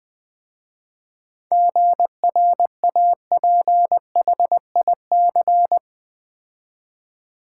Morse code, heard as a keyed tone, about 20 wpm, 690 Hz. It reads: GRAPHIC
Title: 7 Letter Words - First 3 Letters Given - 20wpm